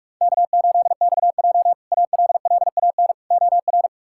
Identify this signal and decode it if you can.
Morse code, heard as a keyed tone, about 47 wpm, 700 Hz. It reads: K8XJ ALLAN OR
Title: K1USN SST QSO Practice - 45wpm